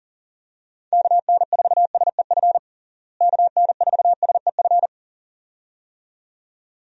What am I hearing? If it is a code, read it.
Morse code, heard as a keyed tone, about 40 wpm, 700 Hz. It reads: KN4SEF KN4SEF
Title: USA Call Signs - 6 Characters - Repeated 2x - 40wpm